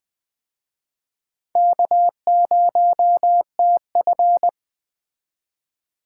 Morse code, heard as a keyed tone, about 21 wpm, 700 Hz. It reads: K0TF